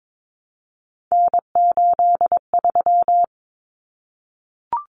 Morse code, heard as a keyed tone, about 22 wpm, 700 Hz. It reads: N83
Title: Common Cut Numbers - 3-Digit 22wpm